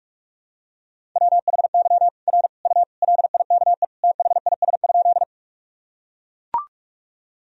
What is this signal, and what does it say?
Morse code, heard as a keyed tone, about 42 wpm, 700 Hz. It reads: WHY R U LIKE THIS?
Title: Sentences from Top 200 Words and Abbreviations 45wpm